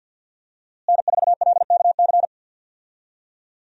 Morse code, heard as a keyed tone, about 51 wpm, 700 Hz. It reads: N4LKC
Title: USA Call Signs - 5 Characters - Repeated 2x - 50wpm